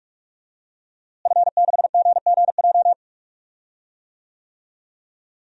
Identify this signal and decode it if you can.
Morse code, heard as a keyed tone, about 46 wpm, 700 Hz. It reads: F6GGJ
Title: DXCC-100 Call Signs - 5 Characters - Repeated 2x - 45wpm